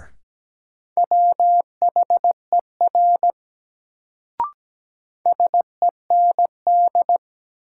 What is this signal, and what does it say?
Morse code, heard as a keyed tone, about 17 wpm, 700 Hz. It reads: WHER SEND
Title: Sentences from Top 1500 Words - New Word Drill Alt - 17wpm